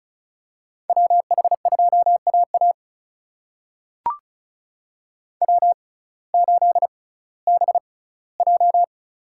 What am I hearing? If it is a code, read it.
Morse code, heard as a keyed tone, about 36 wpm, spaced as for 19 wpm, 700 Hz. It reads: WH2AA W8BJ
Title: USA Call Signs - 6x Character Spacing - 35wpm